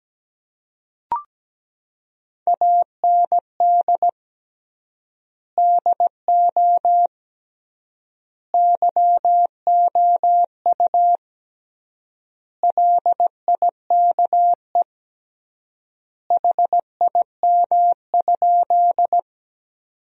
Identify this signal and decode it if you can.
Morse code, heard as a keyed tone, about 17 wpm, 700 Hz. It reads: AND DO YOU LIKE HIM?
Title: 3x Word-Spacing - Sentences from Top 100 Words - 17wpm